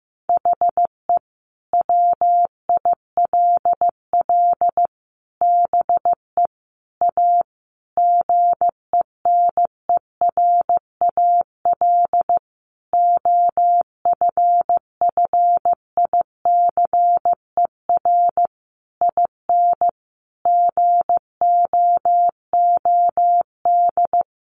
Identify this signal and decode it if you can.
Morse code, heard as a keyed tone, about 15 wpm, 700 Hz. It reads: HE WILL BE A GENERAL OFFICER IN GOOD
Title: Sentences from Top 1200 Words 15wpm